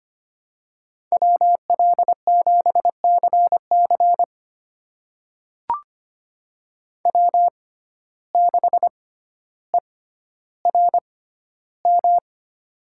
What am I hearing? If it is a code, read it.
Morse code, heard as a keyed tone, about 25 wpm, spaced as for 13 wpm, 700 Hz. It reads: WL7CC W6ERM
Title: USA Call Signs - 5 Characters - 6x Character Spacing - 25wpm